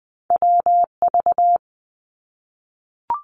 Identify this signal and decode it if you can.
Morse code, heard as a keyed tone, about 19 wpm, 700 Hz. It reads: WV
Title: US States Repeated 3x 20wpm